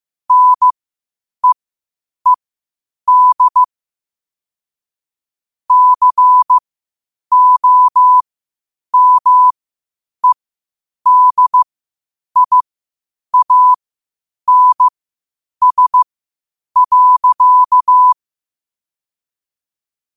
Morse code, heard as a keyed tone, about 15 wpm, spaced as for 8 wpm, 1000 Hz. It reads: NEED COMEDIANS.
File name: qotdfiles / 8wpm